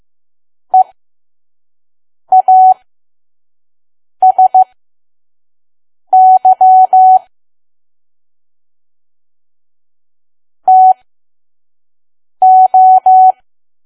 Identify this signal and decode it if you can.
Morse code, heard as a keyed tone, about 15 wpm, spaced as for 5 wpm, 750 Hz. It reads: EASY TO
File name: Archive / 5 WPM files / PM